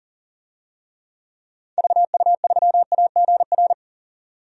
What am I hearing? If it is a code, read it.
Morse code, heard as a keyed tone, about 39 wpm, 700 Hz. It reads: VU3AGR
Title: DXCC-100 Call Signs - 6 Characters - 40wpm